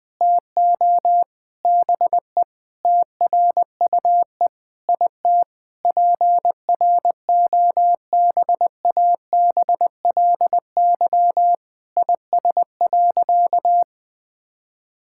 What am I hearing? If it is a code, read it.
Morse code, heard as a keyed tone, about 21 wpm, 700 Hz. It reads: TO BE TRUE IT PROBABLY IS.
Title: Sentences with 9-Word Phrases - 20wpm